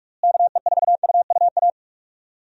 Morse code, heard as a keyed tone, about 43 wpm, 700 Hz. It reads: KE4UUA
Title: USA Call Signs - 6 Characters - Repeated 2x - 45wpm